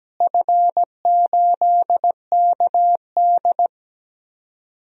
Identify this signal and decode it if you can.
Morse code, heard as a keyed tone, about 17 wpm, 700 Hz. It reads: F8KD
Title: DXCC-100 Call Signs - 4 Characters - Repeated 2x - 17wpm